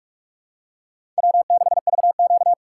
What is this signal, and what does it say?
Morse code, heard as a keyed tone, about 46 wpm, 700 Hz. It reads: W6VQ
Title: USA Call Signs - 4x Character Spacing - 45wpm